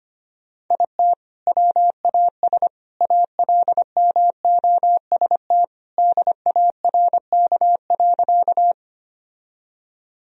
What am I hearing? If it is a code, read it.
Morse code, heard as a keyed tone, about 25 wpm, 700 Hz. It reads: IT WAS ALMOST DARK.
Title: Sentences from Top 1000 Words - Review Part 4 - 25wpm